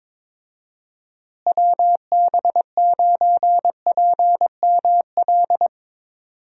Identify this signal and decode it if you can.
Morse code, heard as a keyed tone, about 22 wpm, 700 Hz. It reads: WB9PML
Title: USA Call Signs - 6 Characters - Repeated 2x - 22wpm